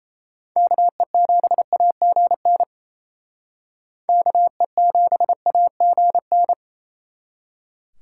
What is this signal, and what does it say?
Morse code, heard as a keyed tone, about 35 wpm, 700 Hz. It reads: KE7AGN KE7AGN
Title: Speed Racing - USA 2x3 Call Signs 28wpm